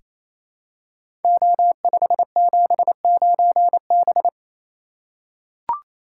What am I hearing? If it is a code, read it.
Morse code, heard as a keyed tone, about 28 wpm, 700 Hz. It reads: O579B
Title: Code Groups - 5 Letter:Numbers - 28wpm